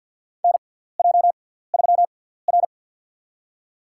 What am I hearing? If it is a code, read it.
Morse code, heard as a keyed tone, about 47 wpm, spaced as for 17 wpm, 700 Hz. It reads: NJ3R
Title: USA Call Signs - 6x Character Spacing - 50wpm